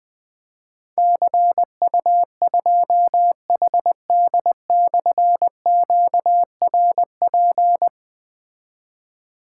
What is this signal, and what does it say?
Morse code, heard as a keyed tone, about 20 wpm, 700 Hz. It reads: CU2HD/QRP
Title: Difficult Call Signs 20wpm